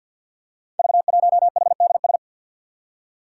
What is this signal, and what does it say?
Morse code, heard as a keyed tone, about 51 wpm, 700 Hz. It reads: V1HDS
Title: Five Challenging Letter-Number Combinations 50wpm